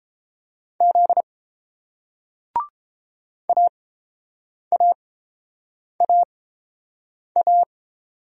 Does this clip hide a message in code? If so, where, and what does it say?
Yes: Morse code from 0.8 to 7.6 s, heard as a keyed tone, about 30 wpm, spaced as for 7 wpm, 700 Hz. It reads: Z AAAA